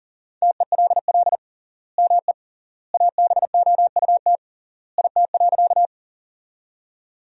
Morse code, heard as a keyed tone, about 40 wpm, 700 Hz. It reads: TELL ME ABOUT IT.
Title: 3x Word-Spacing - Sentences from Top 200 Words - 40wpm